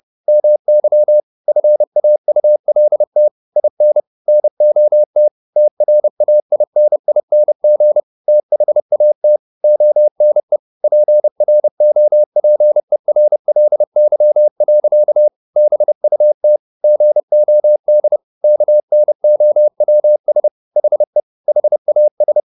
Morse code, heard as a keyed tone, about 30 wpm, 600 Hz. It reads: MY FAULT IN NOT TRAINING THAT ONE PROPERLY. BUT GOD KNOWS HE HAS